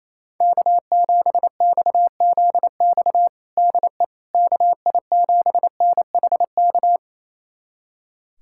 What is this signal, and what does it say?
Morse code, heard as a keyed tone, about 28 wpm, 700 Hz. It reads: K7XZX DE KI7NHK
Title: USA Callsign DE Callsign 28wpm